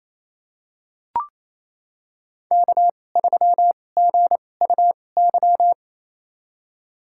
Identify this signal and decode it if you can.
Morse code, heard as a keyed tone, about 28 wpm, spaced as for 20 wpm, 700 Hz. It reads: K3GUY
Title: USA Call Signs - 5 Characters - 2x Character Spacing - 28wpm